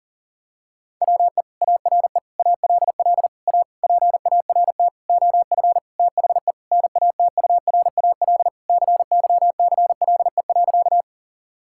Morse code, heard as a keyed tone, about 40 wpm, 700 Hz. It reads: WE ARE ALL A PART OF THE NATURAL CYCLE.
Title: Sentences from Top 1000 Words - Review Part 6 - 40wpm